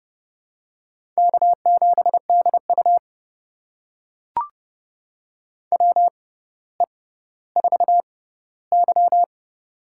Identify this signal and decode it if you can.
Morse code, heard as a keyed tone, about 30 wpm, spaced as for 15 wpm, 700 Hz. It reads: K7DU WE4Y
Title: USA Call Signs - 6x Character Spacing - 30wpm